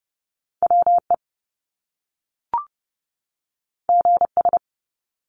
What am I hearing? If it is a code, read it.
Morse code, heard as a keyed tone, about 30 wpm, 700 Hz. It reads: WE GS
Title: Top Two Letter Combinations in Words 30wpm